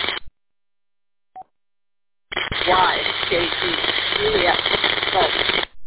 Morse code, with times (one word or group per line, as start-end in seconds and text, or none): none